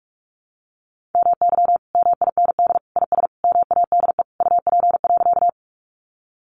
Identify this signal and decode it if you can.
Morse code, heard as a keyed tone, about 44 wpm, 700 Hz. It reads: MY MIND IS MADE UP.
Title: Sentences from Top 500 Words 45wpm